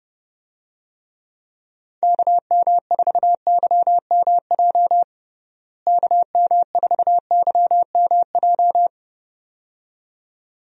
Morse code, heard as a keyed tone, about 30 wpm, 700 Hz. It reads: KM4YMJ KM4YMJ
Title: USA Call Signs - 6 Characters - Repeated 2x - 30wpm